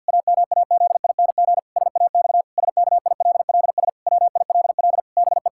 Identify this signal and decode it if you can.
Morse code, heard as a keyed tone, about 50 wpm, 700 Hz. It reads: AMAZING SAX SKILLS WILL BE
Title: Pangram Challenge v2 - 50wpm